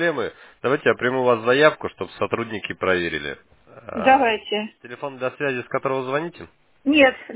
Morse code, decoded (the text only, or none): none